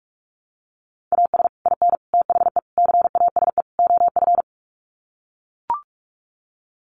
Morse code, heard as a keyed tone, about 44 wpm, 700 Hz. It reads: AS IN THE CASE OF
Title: Sets of 5 Words 45wpm